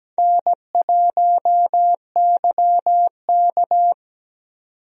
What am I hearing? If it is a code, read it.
Morse code, heard as a keyed tone, about 17 wpm, 700 Hz. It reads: N1YK
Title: USA Call Signs - 4 Characters - 17wpm